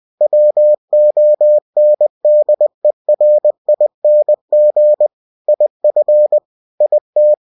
Morse code, heard as a keyed tone, about 20 wpm, 600 Hz. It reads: WONDERING IF IT
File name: cw-books / winnie-the-pooh / winnie-20wpm-page-0035